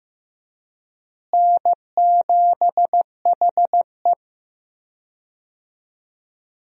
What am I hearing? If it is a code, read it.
Morse code, heard as a keyed tone, about 15 wpm, 700 Hz. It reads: N7HE
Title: USA Call Signs - 4 Characters - Repeated 3x - 15wpm